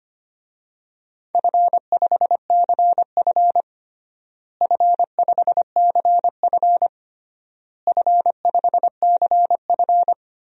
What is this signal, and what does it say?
Morse code, heard as a keyed tone, about 25 wpm, 700 Hz. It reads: F5CF F5CF F5CF
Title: DXCC-100 Call Signs - 4 Characters - Repeated 3x - 25wpm